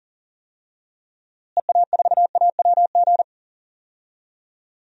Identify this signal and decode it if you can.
Morse code, heard as a keyed tone, about 37 wpm, 700 Hz. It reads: EA4AWG
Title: DXCC-100 Call Signs - 6 Characters - Repeated 3x - 40wpm